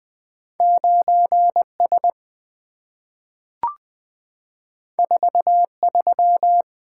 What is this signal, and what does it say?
Morse code, heard as a keyed tone, about 20 wpm, 700 Hz. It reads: 9S 43
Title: Two Letter-Number Combinations - 20wpm